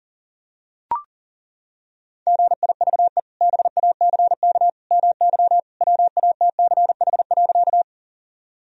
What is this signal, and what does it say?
Morse code, heard as a keyed tone, about 40 wpm, 700 Hz. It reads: GIVE BACK MY WATCH.
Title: Sentences from Top 1000 Words - Review Part 1 - 40wpm